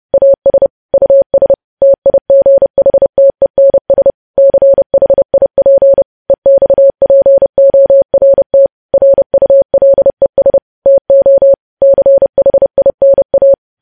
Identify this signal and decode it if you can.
Morse code, heard as a keyed tone, about 30 wpm, 550 Hz. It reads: AS US TIGHTENS CHIP EXPORT RULES TO CHINA
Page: news headlines in morse code at 30 wpm